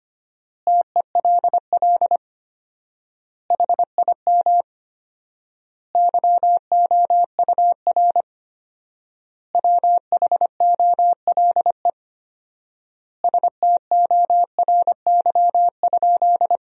Morse code, heard as a keyed tone, about 25 wpm, 700 Hz. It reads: TELL HIM YOUR WHOLE STORY?
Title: 4x Word-Spacing - Sentences from Top 400 Words - 25wpm